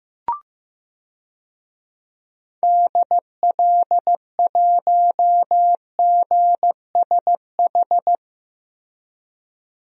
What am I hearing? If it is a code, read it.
Morse code, heard as a keyed tone, about 15 wpm, 700 Hz. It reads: DL1GSH
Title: DXCC-100 Call Signs - 6 Characters - Repeated 2x - 15wpm